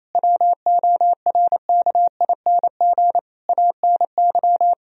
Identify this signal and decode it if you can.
Morse code, heard as a keyed tone, about 28 wpm, 700 Hz. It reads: WORKING ANY
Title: Sentences from Top 1000 Words - Review Part 4 - 28wpm